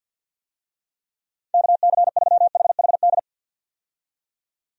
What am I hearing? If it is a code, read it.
Morse code, heard as a keyed tone, about 51 wpm, 700 Hz. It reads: KC3HHD
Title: USA Call Signs - 6 Characters - 50wpm